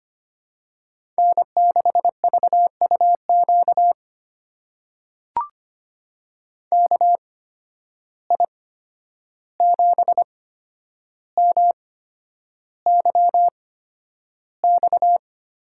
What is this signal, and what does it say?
Morse code, heard as a keyed tone, about 25 wpm, spaced as for 10 wpm, 700 Hz. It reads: N6VUQ KI7MYX
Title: USA Call Signs - 8x Character Spacing - 25wpm